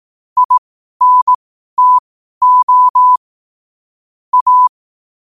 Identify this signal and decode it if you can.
Morse code, heard as a keyed tone, about 18 wpm, spaced as for 12 wpm, 1000 Hz. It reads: INTO A